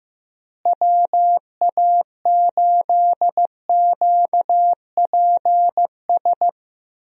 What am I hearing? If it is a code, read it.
Morse code, heard as a keyed tone, about 15 wpm, 700 Hz. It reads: WA8QPS
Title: USA Call Signs - 6 Characters - Repeated 3x - 15wpm